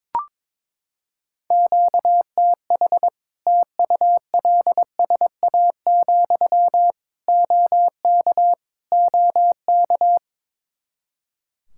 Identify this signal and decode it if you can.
Morse code, heard as a keyed tone, about 22 wpm, 700 Hz. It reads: QTH TULSA, OK OK
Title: Practice QSO - Exchange 1 - Beginning - 22wpm